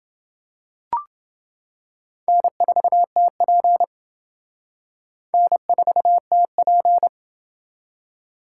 Morse code, heard as a keyed tone, about 28 wpm, 700 Hz. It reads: N4TP N4TP